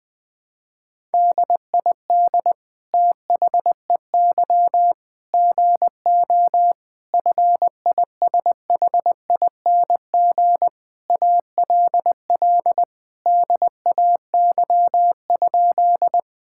Morse code, heard as a keyed tone, about 20 wpm, 700 Hz. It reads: DID THEY GO FISHING ALL DAY?
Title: Sentences from Top 1000 Words - Review Part 5 - 20wpm